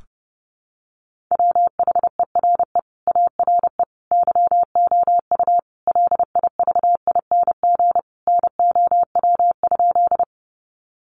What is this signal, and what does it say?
Morse code, heard as a keyed tone, about 30 wpm, 700 Hz. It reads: WHERE ARE YOU LIVING NOW?
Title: Sentences from Top 1100 Words 30wpm